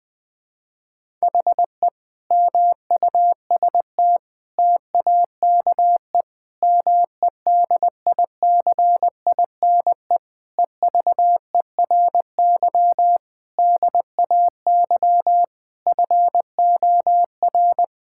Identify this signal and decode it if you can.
Morse code, heard as a keyed tone, about 21 wpm, 700 Hz. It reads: HE MUST TAKE MEDICINE EVERY DAY FOR
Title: Sentences from Top 1000 Words - Review Part 6 - 20wpm